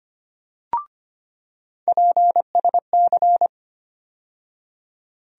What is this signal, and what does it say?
Morse code, heard as a keyed tone, about 25 wpm, 700 Hz. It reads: PSC
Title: USA Call Sign Postfixes - 3 Letters 25wpm